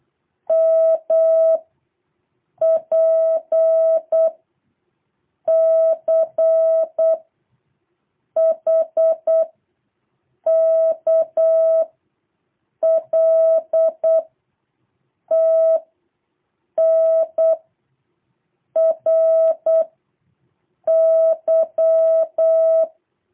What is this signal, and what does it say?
Morse code, heard as a keyed tone, about 8 wpm, 660 Hz. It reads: M P C H K L T N R Y
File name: jh8cbh / 2023cw